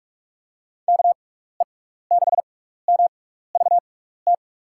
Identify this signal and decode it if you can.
Morse code, heard as a keyed tone, about 46 wpm, spaced as for 16 wpm, 700 Hz. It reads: KE6MVT
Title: USA Call Signs - 6 Characters - 6x Character Spacing - 45wpm